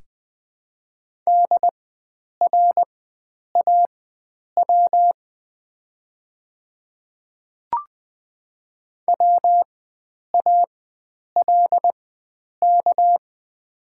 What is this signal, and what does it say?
Morse code, heard as a keyed tone, about 20 wpm, spaced as for 9 wpm, 700 Hz. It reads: DRAW WALK
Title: 4x Character Spacing - Top 300 Words - 20wpm